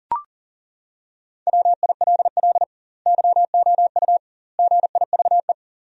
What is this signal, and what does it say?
Morse code, heard as a keyed tone, about 40 wpm, 700 Hz. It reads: WILL YOU GIVE
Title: 2x Word-Spacing - Sentences from Top 200 Words - 40wpm